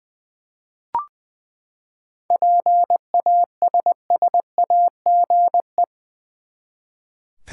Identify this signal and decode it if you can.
Morse code, heard as a keyed tone, about 20 wpm, 700 Hz. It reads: PASSAGE
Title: Seven Letter Words - 20wpm